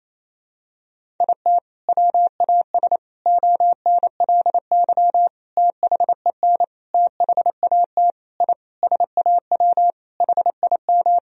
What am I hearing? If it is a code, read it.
Morse code, heard as a keyed tone, about 28 wpm, 700 Hz. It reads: IT WAS ONLY THEN THAT I SAW HIM
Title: Sentences from Top 1000 Words - Review Part 3 - 28wpm